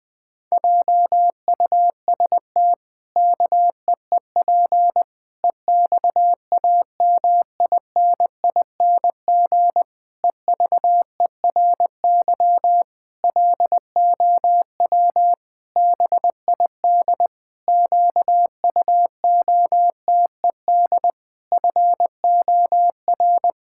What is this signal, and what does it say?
Morse code, heard as a keyed tone, about 20 wpm, 700 Hz. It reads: JUST KEEP EXAMINING EVERY LOW BID QUOTED FOR
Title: Pangram Challenge v2 - 20wpm